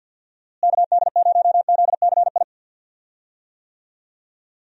Morse code, heard as a keyed tone, about 50 wpm, 700 Hz. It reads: KD0ZCI
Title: USA Call Signs - 6 Characters - Repeated 3x - 50wpm